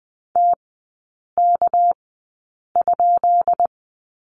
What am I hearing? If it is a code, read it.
Morse code, heard as a keyed tone, about 20 wpm, spaced as for 8 wpm, 700 Hz. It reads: TK?
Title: 218 Lesson 35e - Number 0 - Three Characters 20wpm